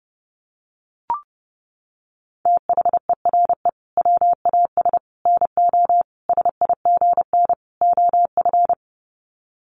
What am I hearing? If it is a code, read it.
Morse code, heard as a keyed tone, about 30 wpm, 700 Hz. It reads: THERE WAS NO SIGN OF
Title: Sets of 5 Words Encore2 30wpm